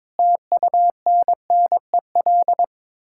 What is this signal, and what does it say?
Morse code, heard as a keyed tone, about 22 wpm, 700 Hz. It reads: TUNNEL
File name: practice / 22wpm / Sets of 8 Words - 22wpm